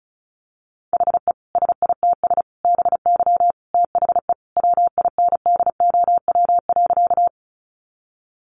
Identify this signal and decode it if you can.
Morse code, heard as a keyed tone, about 34 wpm, 700 Hz. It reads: HE SITS BY THE WINDOW.
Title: Additional Easy Phrases 35wpm